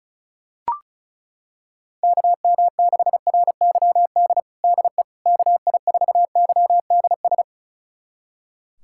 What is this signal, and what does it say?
Morse code, heard as a keyed tone, about 34 wpm, 700 Hz. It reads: KM6RYD DE KI4YDS
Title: USA Callsign DE Callsign - 6 Characters - 35wpm